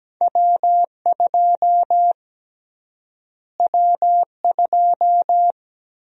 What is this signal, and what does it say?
Morse code, heard as a keyed tone, about 17 wpm, 700 Hz. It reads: W2 W2
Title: USA Call Sign Prefixes - Repeated 2x - 17wpm